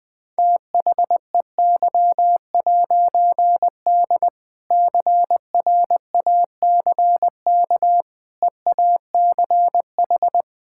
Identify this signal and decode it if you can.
Morse code, heard as a keyed tone, about 21 wpm, 700 Hz. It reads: THEY'D CRACK EACH
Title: All-Age Appropriate Jokes - 20wpm